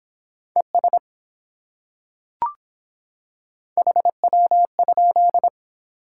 Morse code, heard as a keyed tone, about 25 wpm, 700 Hz. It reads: ES HW?